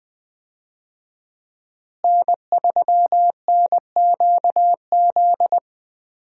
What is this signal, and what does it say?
Morse code, heard as a keyed tone, about 20 wpm, 700 Hz. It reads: N3NQZ
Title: USA Call Signs - 5 Characters - Repeated 3x - 20wpm